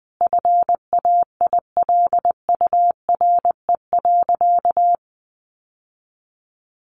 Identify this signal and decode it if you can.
Morse code, heard as a keyed tone, about 19 wpm, 700 Hz. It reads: FAILURE.
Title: Sentences from Top 900 Words 20wpm